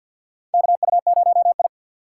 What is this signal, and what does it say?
Morse code, heard as a keyed tone, about 48 wpm, 700 Hz. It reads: KU0I
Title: USA Call Signs - 4 Characters - 6x Character Spacing - 50wpm